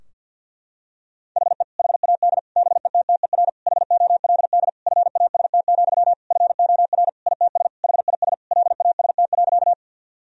Morse code, heard as a keyed tone, about 52 wpm, 700 Hz. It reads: HE HAD BETTER HOLD FAST, FOR ITS HIS LAST.
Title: Sentences from Top 1000 Words - Review Part 2 - 50wpm